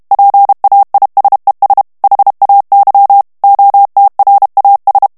Morse code, heard as a keyed tone, about 32 wpm, 800 Hz. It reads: PAISES HAY OTRAS
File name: morse mp3 / ESPAGNOL TEXTE EN CLAIR 1 VITESSE 30 MOTS MINUTES